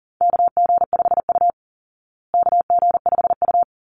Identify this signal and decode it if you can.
Morse code, heard as a keyed tone, about 40 wpm, 700 Hz. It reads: KG5U KG5U
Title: Call Signs Repeated 3x - 40wpm